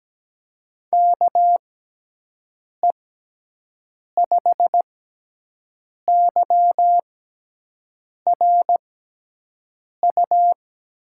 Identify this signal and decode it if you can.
Morse code, heard as a keyed tone, about 17 wpm, 700 Hz. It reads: K E 5 Y R U
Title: USA Call Signs - 6 Characters - 6x Character Spacing - 17wpm